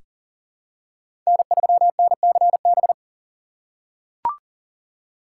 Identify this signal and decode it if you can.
Morse code, heard as a keyed tone, about 40 wpm, 700 Hz. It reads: N3NCB
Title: USA Call Signs - 5 Characters - 2x Character Spacing - 40wpm